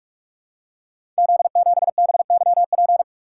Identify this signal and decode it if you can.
Morse code, heard as a keyed tone, about 44 wpm, 700 Hz. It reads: Z7BYP
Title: Code Groups - 5 Letter:Numbers - 45wpm